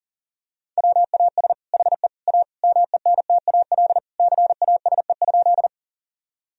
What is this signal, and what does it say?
Morse code, heard as a keyed tone, about 40 wpm, 700 Hz. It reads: WAS HE A MENTAL CASE?
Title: Sentences from Top 1000 Words - Review Part 4 - 40wpm